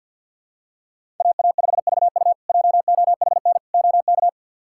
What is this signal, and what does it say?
Morse code, heard as a keyed tone, about 51 wpm, 700 Hz. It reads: AA5VU JOHN OK
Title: K1USN SST QSO Practice - 50wpm